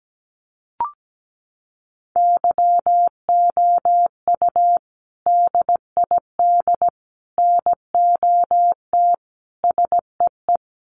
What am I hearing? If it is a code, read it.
Morse code, heard as a keyed tone, about 17 wpm, 700 Hz. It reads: YOU DID NOT SEE
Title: Sentences from Top 900 Words 17wpm